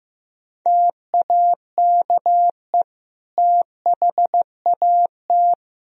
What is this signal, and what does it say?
Morse code, heard as a keyed tone, about 16 wpm, 700 Hz. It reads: TAKE THAT
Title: Sentences from Top 1000 Words - Review Part 2 - 15wpm